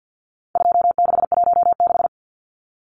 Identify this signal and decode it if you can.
Morse code, heard as a keyed tone, about 50 wpm, 700 Hz. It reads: 2616
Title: Numbers - 4-Digit 50wpm